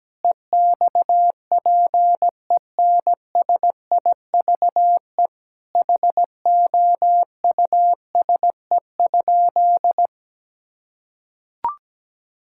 Morse code, heard as a keyed tone, about 17 wpm, 700 Hz. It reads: EXPENSIVE HOUSE?
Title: Sentences from Top 1000 Words - Review Part 6 - 17wpm